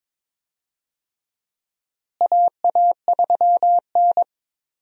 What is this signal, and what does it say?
Morse code, heard as a keyed tone, about 22 wpm, 700 Hz. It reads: AA3N
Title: USA Call Signs - 4 Characters - Repeated 2x - 22wpm